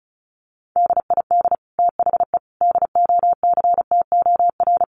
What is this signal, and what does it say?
Morse code, heard as a keyed tone, about 34 wpm, 700 Hz. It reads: DID THE DOCTOR
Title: Sentences from Top 1500 Words 35wpm